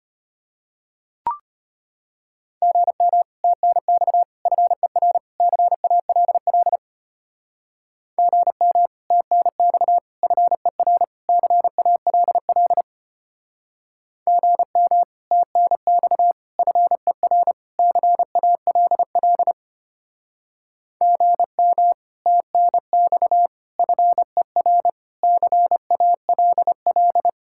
Morse code, heard as a keyed tone, about 30 wpm, 700 Hz. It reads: GM TNX FER CALL GM TNX FER CALL GM TNX FER CALL GM TNX FER CALL
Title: Speed-Racing Common QSO Phrases Encore - 25wpm